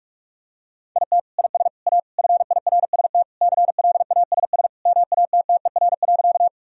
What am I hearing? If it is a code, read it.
Morse code, heard as a keyed tone, about 41 wpm, 700 Hz. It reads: IT IS A FIRST CLASS MATTER.
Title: Sentences from Top 1000 Words - Review Part 2 - 45wpm